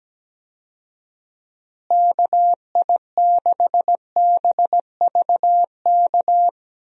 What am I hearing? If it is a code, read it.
Morse code, heard as a keyed tone, about 17 wpm, 700 Hz. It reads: KI6BVK
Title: USA Call Signs - 6 Characters - Repeated 3x - 17wpm